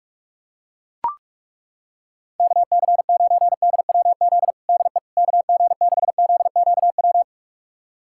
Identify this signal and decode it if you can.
Morse code, heard as a keyed tone, about 42 wpm, 700 Hz. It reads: KC9DWZ DE KG6ZQW
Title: USA Callsign DE Callsign - 6 Characters - 45wpm